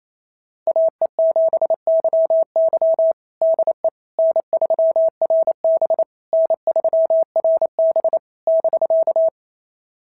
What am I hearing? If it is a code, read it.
Morse code, heard as a keyed tone, about 29 wpm, 690 Hz. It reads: AE7YY DE N3RB N3RB <BK>
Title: SOTA QSO Practice 28wpm